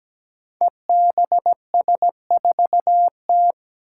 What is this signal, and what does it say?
Morse code, heard as a keyed tone, about 17 wpm, 700 Hz. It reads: EBS4T
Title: Five Challenging Letter-Number Combinations 17wpm